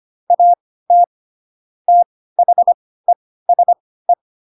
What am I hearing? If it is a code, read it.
Morse code, heard as a keyed tone, about 25 wpm, spaced as for 16 wpm, 700 Hz. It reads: AT THESE
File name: Pinocchio / Pinocchio ch4 10 25 ik1hge